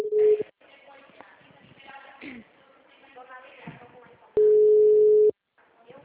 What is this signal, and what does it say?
Ringback tone (Europe, 425 Hz), with speech.
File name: out- -2015- - - .32231